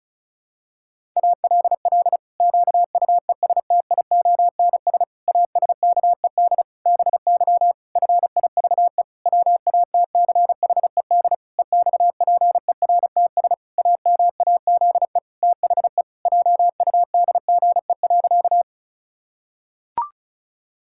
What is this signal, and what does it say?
Morse code, heard as a keyed tone, about 36 wpm, 700 Hz. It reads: ALL QUESTIONS ASKED BY FIVE WATCHED EXPERTS AMAZE THE JUDGE.
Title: Pangram Challenge v2 - 35wpm